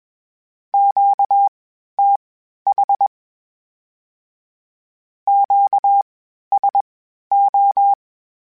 Morse code, heard as a keyed tone, about 21 wpm, spaced as for 12 wpm, 800 Hz. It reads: QTH QSO
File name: (spaced) / Q-Calls Spaced-mp3s / Q Calls sp Once @20